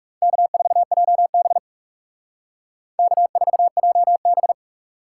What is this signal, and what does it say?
Morse code, heard as a keyed tone, about 38 wpm, 700 Hz. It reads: K4JB K4JB